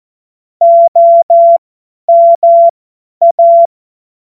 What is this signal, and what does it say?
Morse code, heard as a keyed tone, about 14 wpm, spaced as for 10 wpm, 680 Hz. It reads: OMA
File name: Morsekurs / Witze / Witz 07. März 2023 50 F70